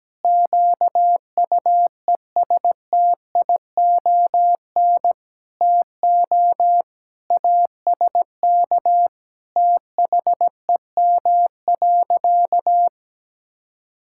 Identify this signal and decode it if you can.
Morse code, heard as a keyed tone, about 17 wpm, 700 Hz. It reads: QUESTION TO ASK THEM.
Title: Sentences from Top 1000 Words - Review Part 6 - 17wpm